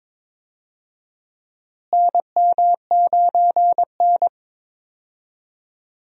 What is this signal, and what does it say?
Morse code, heard as a keyed tone, about 22 wpm, 700 Hz. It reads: NM9N
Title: USA Call Signs - 4 Characters - Repeated 2x - 22wpm